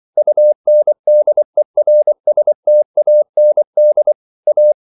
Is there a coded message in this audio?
Yes: Morse code from 0.2 to 4.7 s, heard as a keyed tone, about 24 wpm, 600 Hz. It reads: UNDERSTAND A